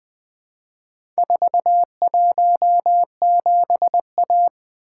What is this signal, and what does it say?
Morse code, heard as a keyed tone, about 20 wpm, 700 Hz. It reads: 417A
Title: Four Letter-Number Combinations - 20wpm